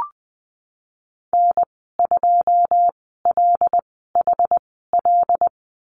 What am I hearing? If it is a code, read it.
Morse code, heard as a keyed tone, about 20 wpm, spaced as for 15 wpm, 700 Hz. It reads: N2LHL
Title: 122 Lesson 20g - Number 3 - Call Signs 20wpm